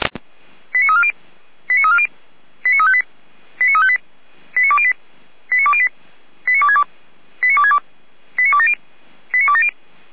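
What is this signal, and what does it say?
Five-tone selective call (ZVEI), heard as quick runs of five tones, tones of about 70 ms. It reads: 89340 89340 89358 89358 89208 89208 89262 89262 89270 89270
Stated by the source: ZVEI five-tone selective call 89305 received via SDR, archived by Feuerwehr Orlamünde.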